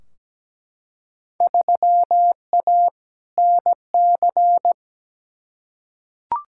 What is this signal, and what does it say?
Morse code, heard as a keyed tone, about 17 wpm, 700 Hz. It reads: 3A NC
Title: Field Day Exchanges - 17wpm